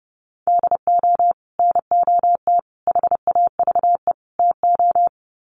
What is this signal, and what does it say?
Morse code, heard as a keyed tone, about 30 wpm, 700 Hz. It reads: DO NOT HAVE TO
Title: Sets of 4 Words 30wpm